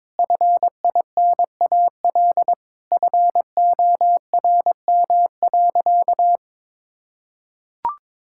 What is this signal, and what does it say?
Morse code, heard as a keyed tone, about 21 wpm, 700 Hz. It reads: FINAL FORM.
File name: Sentences with 7-Word Phrases - 22wpm